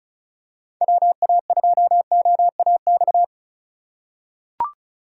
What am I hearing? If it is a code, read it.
Morse code, heard as a keyed tone, about 34 wpm, 700 Hz. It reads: WA2OAX
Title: USA Call Signs - 6 Characters - 4x Character Spacing - 35wpm